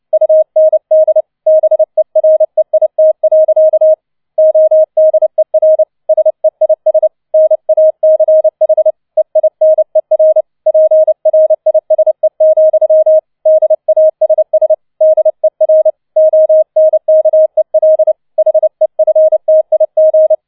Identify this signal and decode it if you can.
Morse code, heard as a keyed tone, about 29 wpm, 630 Hz. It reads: UND BEREIT. ODER SEIS NACH EINER PRISE, DASS DER ONKEL HEFTIG